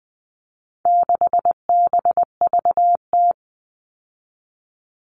Morse code, heard as a keyed tone, about 20 wpm, 700 Hz. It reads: 6BVT
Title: All Cut Numbers - 4-Digit 20wpm